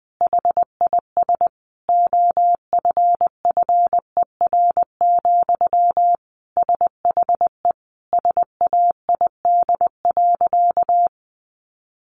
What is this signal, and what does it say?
Morse code, heard as a keyed tone, about 20 wpm, 700 Hz. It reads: HIS OFFER, SHE SAID.
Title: Sentences from Top 1300 Words 20wpm